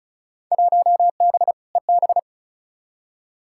Morse code, heard as a keyed tone, about 36 wpm, 690 Hz. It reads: 1B EB
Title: Field Day Exchanges Encore - 35wpm